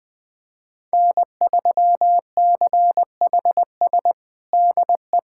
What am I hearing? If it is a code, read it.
Morse code, heard as a keyed tone, about 20 wpm, 700 Hz. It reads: N3CHS DE
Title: USA Callsign DE Callsign - 5 Characters - 20wpm